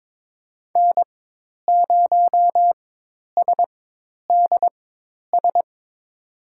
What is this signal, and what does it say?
Morse code, heard as a keyed tone, about 21 wpm, spaced as for 10 wpm, 700 Hz. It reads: N0SDS
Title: USA Call Signs - 5 Characters - 4x Character Spacing - 22wpm